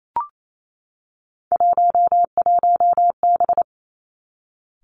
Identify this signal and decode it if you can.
Morse code, heard as a keyed tone, about 28 wpm, 700 Hz. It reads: 11B
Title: Three Challenging Letter-Number Combinations 28wpm